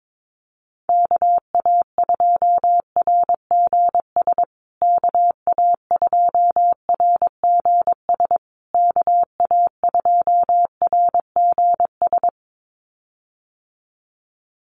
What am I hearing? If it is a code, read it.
Morse code, heard as a keyed tone, about 22 wpm, 700 Hz. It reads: KA2RGS KA2RGS KA2RGS
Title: USA Call Signs Repeated 3x 22wpm